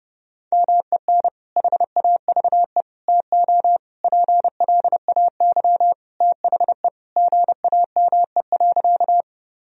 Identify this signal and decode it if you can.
Morse code, heard as a keyed tone, about 31 wpm, 700 Hz. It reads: MEN HAVE TO PLAY THE GAME.
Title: Sentences from Top 1000 Words - Review Part 3 - 30wpm